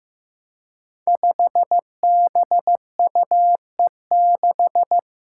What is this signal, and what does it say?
Morse code, heard as a keyed tone, about 15 wpm, 700 Hz. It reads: 5BUE6
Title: Five Challenging Letter-Number Combinations 15wpm